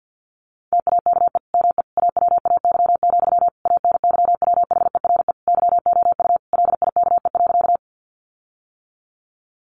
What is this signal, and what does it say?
Morse code, heard as a keyed tone, about 50 wpm, 700 Hz. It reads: TAKE ME AWAY, ANYWHERE YOU LIKE.
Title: Sentences from Top 1400 Words 50wpm